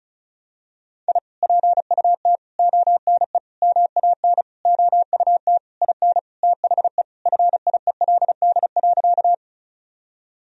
Morse code, heard as a keyed tone, about 34 wpm, 700 Hz. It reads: I PUT ONE MAN OUT IN THE FIELD.
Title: Sentences from Top 1000 Words - Review Part 2 - 35wpm